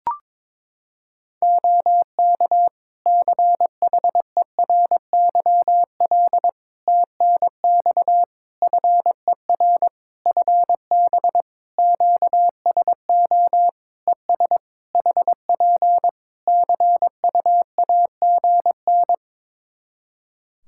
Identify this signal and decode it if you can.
Morse code, heard as a keyed tone, about 22 wpm, 700 Hz. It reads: OK CHERYL TNX FER FB QSO ES HP CUAGN
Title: Practice QSO - Exchange 4 - Ending - 22wpm